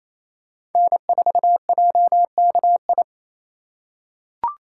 Morse code, heard as a keyed tone, about 28 wpm, 700 Hz. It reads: N4JKI
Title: USA Call Signs - 5 Characters - 4x Character Spacing - 28wpm